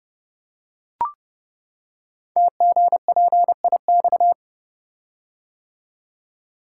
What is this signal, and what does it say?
Morse code, heard as a keyed tone, about 30 wpm, 690 Hz. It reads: TGPIX
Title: Code Groups - 5 Letters - 30wpm